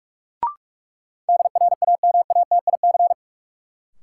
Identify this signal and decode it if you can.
Morse code, heard as a keyed tone, about 42 wpm, 700 Hz. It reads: DRAMATIC
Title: Sentences from Top 1300 Words - New Word Drill Alt - 45wpm